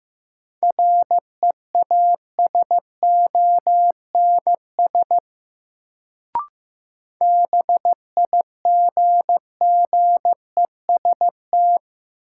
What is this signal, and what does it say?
Morse code, heard as a keyed tone, about 15 wpm, 700 Hz. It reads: REASONS BIGGEST
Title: Sets of 4 Words - New Word Drill - 15wpm